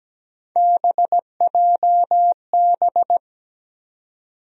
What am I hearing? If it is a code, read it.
Morse code, heard as a keyed tone, about 17 wpm, 700 Hz. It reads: BJB
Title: USA Call Sign Postfixes - 3 Letters 17wpm